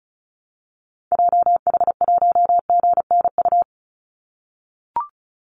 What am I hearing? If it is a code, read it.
Morse code, heard as a keyed tone, about 36 wpm, 700 Hz. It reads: JH1GNU
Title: Call Signs - 35wpm